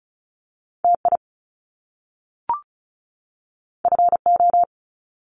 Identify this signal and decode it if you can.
Morse code, heard as a keyed tone, about 36 wpm, 700 Hz. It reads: TI FO
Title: Top Two Letter Combinations in Words 35wpm